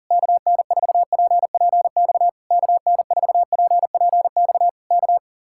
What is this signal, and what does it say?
Morse code, heard as a keyed tone, about 40 wpm, 700 Hz. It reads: KN4PPX KN4PPX K